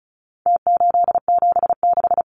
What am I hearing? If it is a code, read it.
Morse code, heard as a keyed tone, about 34 wpm, 700 Hz. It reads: T876
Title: Numbers - Contest Style - 4-Digit 35wpm